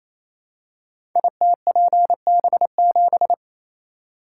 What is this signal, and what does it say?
Morse code, heard as a keyed tone, about 28 wpm, 700 Hz. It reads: ITPB7
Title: Code Groups - 5 Letter:Numbers - 28wpm